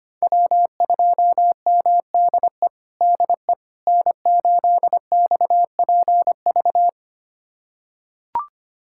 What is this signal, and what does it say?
Morse code, heard as a keyed tone, about 25 wpm, 700 Hz. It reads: W2MDE DE N8XPV
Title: USA Callsign DE Callsign - 5 Characters - 25wpm